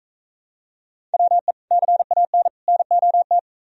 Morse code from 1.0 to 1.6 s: WE

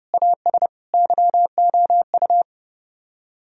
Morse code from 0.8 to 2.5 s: YOU

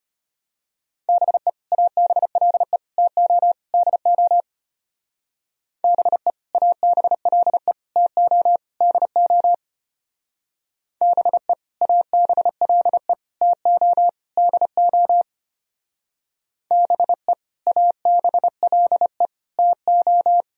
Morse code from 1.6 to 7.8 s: ABLE TO DO BE ABLE

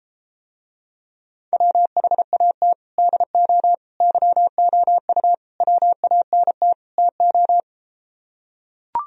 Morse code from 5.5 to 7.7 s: WANT TO

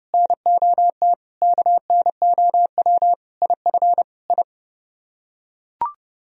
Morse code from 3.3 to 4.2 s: IF

